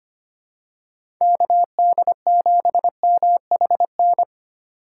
Morse code, heard as a keyed tone, about 25 wpm, 700 Hz. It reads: KD7MHN
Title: USA Call Signs - 2x Character Spacing - 25wpm